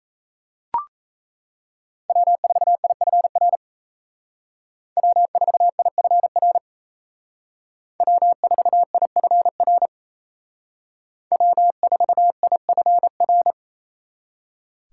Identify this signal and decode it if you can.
Morse code, heard as a keyed tone, about 33 wpm, 700 Hz. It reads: W4IFR W4IFR W4IFR W4IFR
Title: Speed Racing - USA 1x3 Call Signs 28wpm